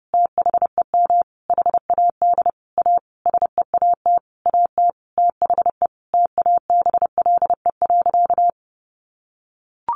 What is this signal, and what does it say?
Morse code, heard as a keyed tone, about 30 wpm, 700 Hz. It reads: THEM HAD A SEAT AT THE TABLE.
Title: Sentences from Top 1500 Words 30wpm